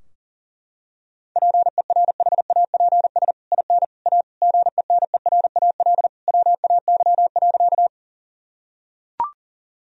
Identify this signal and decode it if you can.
Morse code, heard as a keyed tone, about 40 wpm, 700 Hz. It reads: PERHAPS IN A GENERAL WAY.
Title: Sentences from Top 1000 Words - Review Part 3 - 40wpm